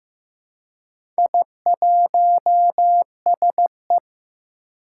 Morse code, heard as a keyed tone, about 15 wpm, 700 Hz. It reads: I1SE
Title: Four Challenging Letter-Number Combinations 15wpm